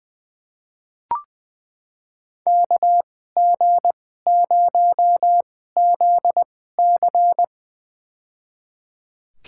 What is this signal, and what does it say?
Morse code, heard as a keyed tone, about 20 wpm, spaced as for 15 wpm, 700 Hz. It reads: KG0ZC
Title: USA Call Signs - 5 Characters - 2x Character Spacing - 20wpm